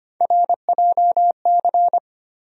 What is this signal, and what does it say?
Morse code, heard as a keyed tone, about 25 wpm, 700 Hz. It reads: RJC
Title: USA Call Sign Postfixes - 3 Letters 25wpm